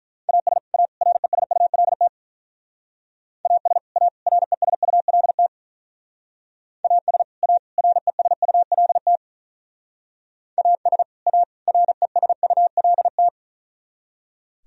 Morse code, heard as a keyed tone, about 44 wpm, 700 Hz. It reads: AS A RESULT AS A RESULT AS A RESULT AS A RESULT
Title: Speed-Racing - Sets of 3 Words 35wpm